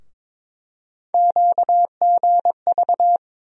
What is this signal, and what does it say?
Morse code, heard as a keyed tone, about 22 wpm, 700 Hz. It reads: QGV
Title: USA Call Sign Postfixes - 3 Letters 22wpm